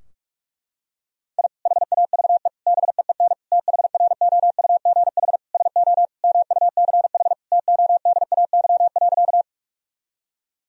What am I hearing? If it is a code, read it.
Morse code, heard as a keyed tone, about 42 wpm, 700 Hz. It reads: I HAVE BEEN THROUGH SO MUCH TODAY.
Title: Sentences from Top 1000 Words - Review Part 4 - 45wpm